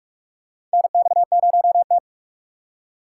Morse code, heard as a keyed tone, about 47 wpm, 700 Hz. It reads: NX0T
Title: USA Call Signs - 4 Characters - 4x Character Spacing - 45wpm